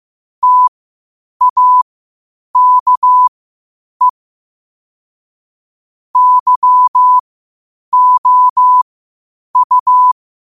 Morse code, heard as a keyed tone, about 15 wpm, spaced as for 8 wpm, 1000 Hz. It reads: TAKE YOU
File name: qotdfiles / 8wpm